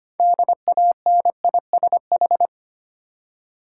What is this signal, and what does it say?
Morse code, heard as a keyed tone, about 25 wpm, 700 Hz. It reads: DANISH
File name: practice / 25wpm / Difficult Words with Hints - 25wpm